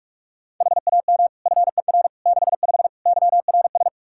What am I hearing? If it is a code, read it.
Morse code, heard as a keyed tone, about 46 wpm, 700 Hz. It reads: HAM FER 65 YRS
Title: Practice QSO - Exchange 3 - More Info - 45wpm